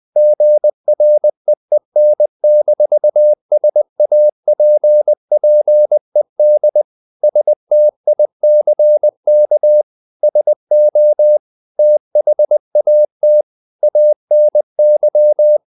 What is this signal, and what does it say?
Morse code, heard as a keyed tone, about 20 wpm, 600 Hz. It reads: GREEN-SAPPED STICK SO THAT ANY
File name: the old man and the sea / the old man and the sea part 1 20wpm